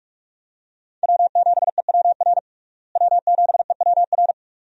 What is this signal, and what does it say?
Morse code, heard as a keyed tone, about 46 wpm, 700 Hz. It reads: W7EWR W7EWR
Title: USA Call Signs - 5 Characters - Repeated 2x - 45wpm